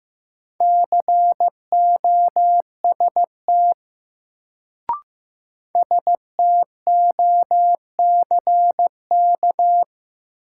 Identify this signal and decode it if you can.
Morse code, heard as a keyed tone, about 15 wpm, 700 Hz. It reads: COST STOCK
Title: Sentences from Top 800 Words - New Word Drill - 15wpm